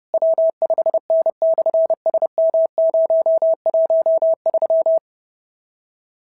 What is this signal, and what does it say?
Morse code, heard as a keyed tone, about 30 wpm, 690 Hz. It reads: W5N/SM013
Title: SOTA QSO Practice 30wpm